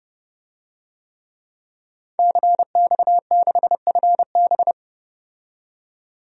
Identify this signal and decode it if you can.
Morse code, heard as a keyed tone, about 30 wpm, 700 Hz. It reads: CX6FB
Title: DXCC-100 Call Signs - 5 Characters - 30wpm